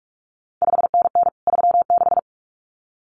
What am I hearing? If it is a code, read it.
Morse code, heard as a keyed tone, about 46 wpm, 700 Hz. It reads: 5NN 36
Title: CQ WW DX Contest Exchange Encore - 45wpm